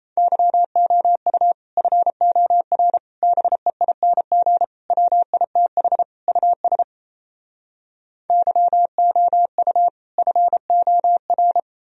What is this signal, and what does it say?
Morse code, heard as a keyed tone, about 38 wpm, 700 Hz. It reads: YOU FOR BEING WITH US YOU FOR